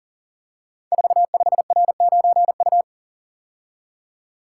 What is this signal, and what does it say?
Morse code, heard as a keyed tone, about 40 wpm, 700 Hz. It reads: 45R9U
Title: Code Groups - 5 Letter:Numbers - 40wpm